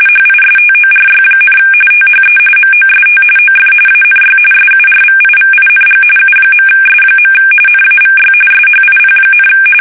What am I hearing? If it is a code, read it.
RTTY, about 45.45 baud. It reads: HISTORY, THE MOST OF ANY SURVIVING U.S. NAVY SHIP,
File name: fsk850 / battleshipnj